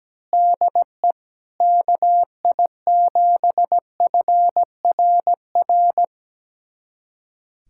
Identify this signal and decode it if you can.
Morse code, heard as a keyed tone, about 17 wpm, 700 Hz. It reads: DE KI7FRR
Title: USA Callsign DE Callsign 17wpm